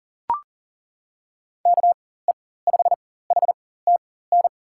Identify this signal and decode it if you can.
Morse code, heard as a keyed tone, about 40 wpm, spaced as for 19 wpm, 700 Hz. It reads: KE5HTN
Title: USA Call Signs - 6 Characters - 4x Character Spacing - 40wpm